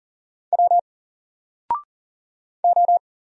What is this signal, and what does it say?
Morse code, heard as a keyed tone, about 40 wpm, 700 Hz. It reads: W O
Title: Single Letter-Number 40wpm